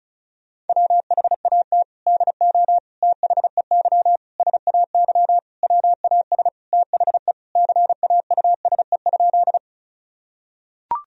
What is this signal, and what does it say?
Morse code, heard as a keyed tone, about 36 wpm, 700 Hz. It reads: WHAT DO THEY SAY WAS THE CAUSE?
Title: Sentences from Top 1000 Words - Review Part 1 - 35wpm